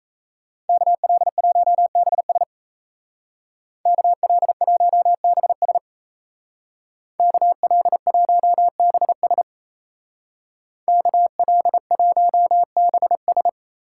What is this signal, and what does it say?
Morse code, heard as a keyed tone, about 34 wpm, 700 Hz. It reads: KL1BS KL1BS KL1BS KL1BS
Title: Speed Racing - USA 2x2 Call Signs 28wpm